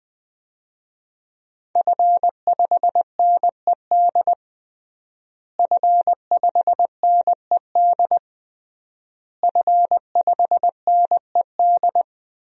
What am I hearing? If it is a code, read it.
Morse code, heard as a keyed tone, about 20 wpm, 700 Hz. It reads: F5NED F5NED F5NED
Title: DXCC-100 Call Signs - 5 Characters - Repeated 3x - 20wpm